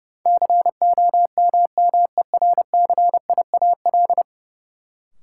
Morse code, heard as a keyed tone, about 32 wpm, 700 Hz. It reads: COMMERCIAL
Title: Sentences from Top 1000 Words - Review Part 5 - New Word Drill - 30wpm